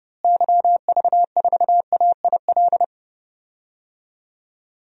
Morse code, heard as a keyed tone, about 30 wpm, 700 Hz. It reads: YV4AIL
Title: DXCC-100 Call Signs - 6 Characters - Repeated 2x - 30wpm